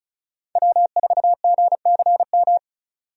Morse code, heard as a keyed tone, about 36 wpm, 700 Hz. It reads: W4GCM
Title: USA Call Signs - 5 Characters - 35wpm